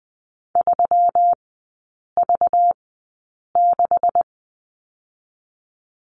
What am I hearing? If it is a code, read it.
Morse code, heard as a keyed tone, about 20 wpm, spaced as for 8 wpm, 690 Hz. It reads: 3V6
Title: 212 Lesson 34e - Number 8 - Three Characters 20wpm